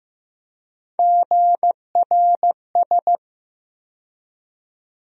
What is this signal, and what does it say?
Morse code, heard as a keyed tone, about 15 wpm, 700 Hz. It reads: GRS
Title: USA Call Sign Postfixes - 3 Letters 15wpm